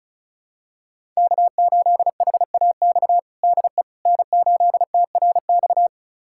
Morse code, heard as a keyed tone, about 36 wpm, 700 Hz. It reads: K8HAX DE N8TRX
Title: USA Callsign DE Callsign - 5 Characters - 35wpm